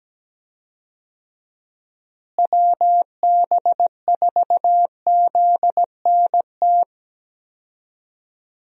Morse code, heard as a keyed tone, about 17 wpm, 700 Hz. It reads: WB4ZNT
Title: USA Call Signs - 6 Characters - Repeated 2x - 17wpm